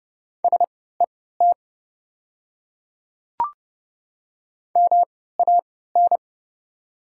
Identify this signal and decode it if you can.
Morse code, heard as a keyed tone, about 30 wpm, spaced as for 17 wpm, 700 Hz. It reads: SET MAN
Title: 3x Character Spacing - Top 200 Words - 30wpm